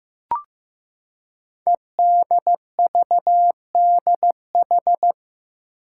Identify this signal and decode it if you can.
Morse code, heard as a keyed tone, about 15 wpm, 700 Hz. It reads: EDVDH
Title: Code Groups - 5 Letters - 15wpm